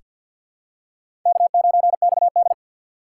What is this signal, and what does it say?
Morse code, heard as a keyed tone, about 47 wpm, 700 Hz. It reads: K9XD
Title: K1USN SST QSO Practice - 50wpm